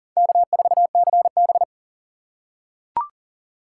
Morse code, heard as a keyed tone, about 40 wpm, 700 Hz. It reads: K4CB
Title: USA Call Signs - 3x Character Spacing - 40wpm